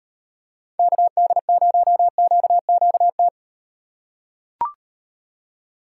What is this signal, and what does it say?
Morse code, heard as a keyed tone, about 35 wpm, 700 Hz. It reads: KD0QQT